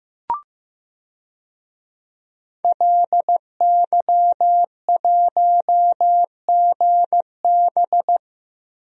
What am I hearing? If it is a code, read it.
Morse code, heard as a keyed tone, about 15 wpm, 700 Hz. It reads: LY1GB
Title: DXCC-100 Call Signs - 5 Characters - Repeated 3x - 15wpm